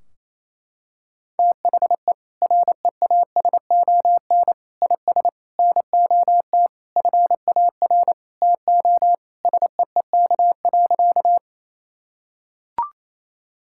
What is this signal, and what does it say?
Morse code, heard as a keyed tone, about 27 wpm, 700 Hz. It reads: THE REASON IS NOT FAR TO SEEK.
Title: Sentences from Top 1600 Words 28wpm